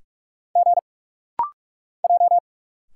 Morse code, heard as a keyed tone, about 46 wpm, 700 Hz. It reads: G J
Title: Single Letter-Number - Rapid-Fire 45wpm